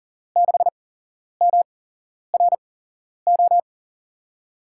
Morse code, heard as a keyed tone, about 42 wpm, 700 Hz. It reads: 6 M R O
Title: USA Call Signs - 5 Characters - 8x Character Spacing - 40wpm